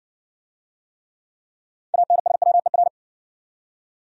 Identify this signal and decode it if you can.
Morse code, heard as a keyed tone, about 56 wpm, 700 Hz. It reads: ANSWER